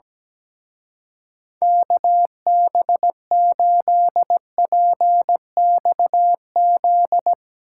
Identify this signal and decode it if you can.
Morse code, heard as a keyed tone, about 17 wpm, 700 Hz. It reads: KB8PXZ
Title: Speed Racing - USA 2x3 Call Signs 17wpm